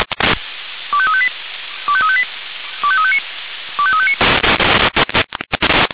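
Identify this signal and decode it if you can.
Five-tone selective call (ZVEI), heard as quick runs of five tones, tones of about 70 ms. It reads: 25358 25358 25359 25359
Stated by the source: ZVEI five-tone selective call 25354 received by SDR.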